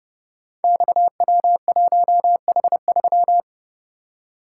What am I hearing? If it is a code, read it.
Morse code, heard as a keyed tone, about 30 wpm, 700 Hz. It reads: XW1H3
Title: Code Groups - 5 Letter:Numbers - 30wpm